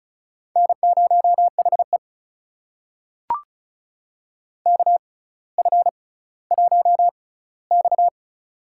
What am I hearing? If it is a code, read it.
Morse code, heard as a keyed tone, about 36 wpm, spaced as for 18 wpm, 700 Hz. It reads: N0HE KF1X
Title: USA Call Signs - 4 Characters - 6x Character Spacing - 35wpm